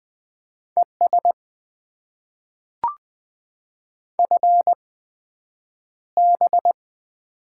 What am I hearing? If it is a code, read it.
Morse code, heard as a keyed tone, about 20 wpm, 700 Hz. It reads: E S F B
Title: Top 20 QSO Elements - 8x Character Spacing - 20wpm